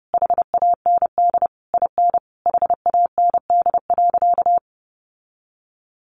Morse code, heard as a keyed tone, about 30 wpm, 700 Hz. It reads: HAND IN HAND.
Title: Sentences from Top 1200 Words 30wpm